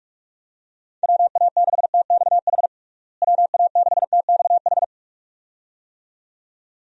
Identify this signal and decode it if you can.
Morse code, heard as a keyed tone, about 46 wpm, 700 Hz. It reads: WA6TXH WA6TXH
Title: USA Call Signs - 6 Characters - Repeated 2x - 45wpm